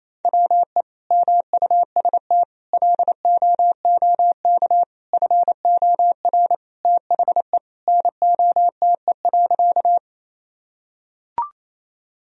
Morse code, heard as a keyed tone, about 27 wpm, 700 Hz. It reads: WE MUST LOOK FOR THE NOTE.
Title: Sentences from Top 1000 Words - Review Part 2 - 28wpm